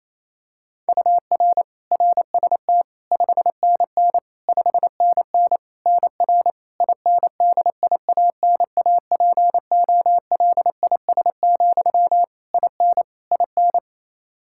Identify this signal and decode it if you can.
Morse code, heard as a keyed tone, about 27 wpm, 700 Hz. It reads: UR RST 5NN 5NN NR INDIANAPOLIS, IN IN
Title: K1USN SST QSO Practice 28wpm